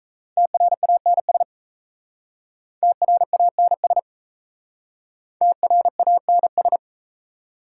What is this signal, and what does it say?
Morse code, heard as a keyed tone, about 38 wpm, 700 Hz. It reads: TRANS TRANS TRANS